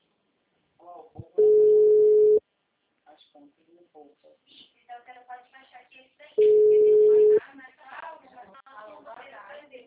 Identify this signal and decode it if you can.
Ringback tone (Europe, 425 Hz), with speech.
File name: out- -2006- - - .88023